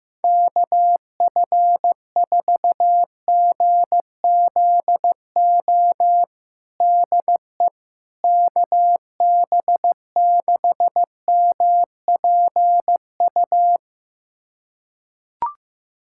Morse code, heard as a keyed tone, about 15 wpm, 700 Hz. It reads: KF4GZO DE KB6MPU
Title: USA Callsign DE Callsign - 6 Characters - 15wpm